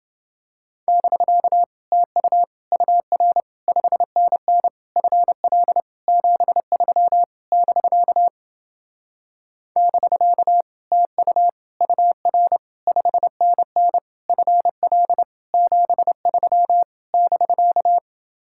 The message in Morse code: <BK> TU UR 5NN FL 73 <BK> <BK> TU UR 5NN FL 73 <BK>

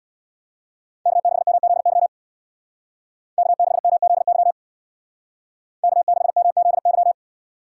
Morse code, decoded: K6MZY K6MZY K6MZY

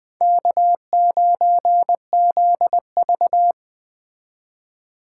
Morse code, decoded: K9ZV

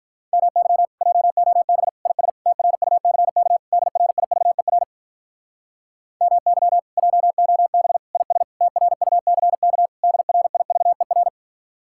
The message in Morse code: MY JOB IS TRUCK DRIVER MY JOB IS TRUCK DRIVER

